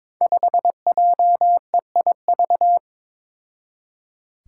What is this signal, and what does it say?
Morse code, heard as a keyed tone, about 21 wpm, 700 Hz. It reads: 5JEIV
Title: Five Challenging Letter-Number Combinations 22wpm